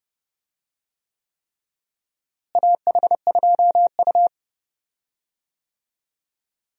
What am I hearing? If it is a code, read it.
Morse code, heard as a keyed tone, about 30 wpm, 700 Hz. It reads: AH2U
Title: USA Call Signs - 4 Characters - Repeated 3x - 30wpm